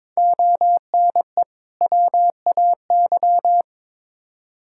Morse code, heard as a keyed tone, about 21 wpm, 700 Hz. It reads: ONE WAY
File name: practice / 22wpm / Build A Phrase - 5-Word Phrases - 22wpm